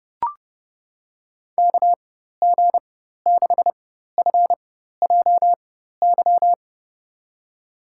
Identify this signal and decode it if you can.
Morse code, heard as a keyed tone, about 30 wpm, spaced as for 14 wpm, 700 Hz. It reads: KG6FJY
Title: USA Call Signs - 6 Characters - 4x Character Spacing - 30wpm